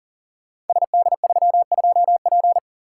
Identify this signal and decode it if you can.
Morse code, heard as a keyed tone, about 39 wpm, 690 Hz. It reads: SD32P
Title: Code Groups - 5 Letter:Numbers - 40wpm